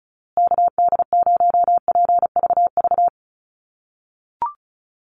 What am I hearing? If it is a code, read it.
Morse code, heard as a keyed tone, about 34 wpm, 700 Hz. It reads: KD0PVV
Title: USA Call Signs Repeated 3x 35wpm